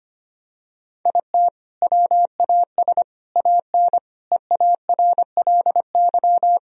Morse code, heard as a keyed tone, about 25 wpm, 700 Hz. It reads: IT WAS AN EARLY
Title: Sentences from Top 1000 Words - Review Part 4 - 25wpm